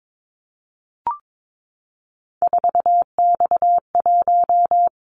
Morse code, heard as a keyed tone, about 22 wpm, 700 Hz. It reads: 4X1
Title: Callsign Prefixes for Top 50 DXCC - 22wpm